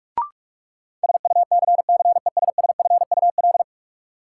Morse code, heard as a keyed tone, about 46 wpm, 700 Hz. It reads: SUCCESSFUL
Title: Sentences from Top 900 Words - New Word Drill - 45wpm